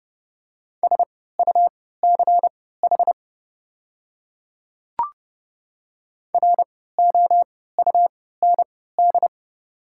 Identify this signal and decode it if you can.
Morse code, heard as a keyed tone, about 30 wpm, spaced as for 17 wpm, 700 Hz. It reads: SUCH ROUND
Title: 3x Character Spacing - Top 300 Words - 30wpm